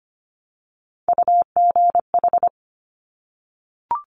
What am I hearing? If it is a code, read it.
Morse code, heard as a keyed tone, about 25 wpm, 700 Hz. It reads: UGH
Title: Top Three Letter Combinations in Words 25wpm